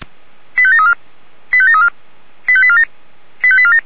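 Five-tone selective call (ZVEI), heard as quick runs of five tones, tones of about 70 ms. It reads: 85734 85734 85847 85847
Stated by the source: ZVEI five-tone selective call 85748 received with an SDR.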